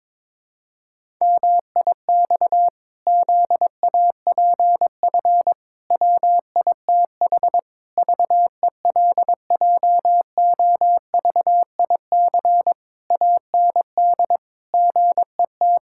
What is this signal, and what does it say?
Morse code, heard as a keyed tone, about 22 wpm, 700 Hz. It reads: MIX ZAPF WITH VELJOVIC AND GET
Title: Pangram Challenge v2 - 22wpm